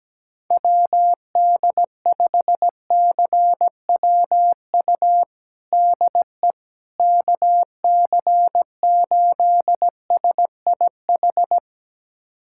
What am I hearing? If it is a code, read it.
Morse code, heard as a keyed tone, about 17 wpm, 700 Hz. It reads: WD5CWU DE KC8SIH
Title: USA Callsign DE Callsign 17wpm